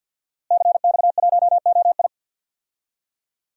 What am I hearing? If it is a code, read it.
Morse code, heard as a keyed tone, about 50 wpm, 700 Hz. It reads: CX1OI
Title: DXCC-100 Call Signs - 5 Characters - Repeated 2x - 50wpm